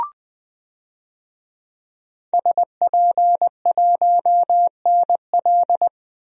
Morse code, heard as a keyed tone, about 20 wpm, 700 Hz. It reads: SP1NL
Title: DXCC-100 Call Signs - 5 Characters - 20wpm